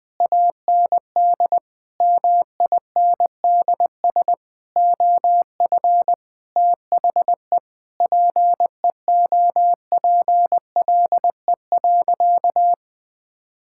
Morse code, heard as a keyed tone, about 20 wpm, 700 Hz. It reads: AND MINDS OF THE PEOPLE.
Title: Sentences with 8-Word Phrases - 20wpm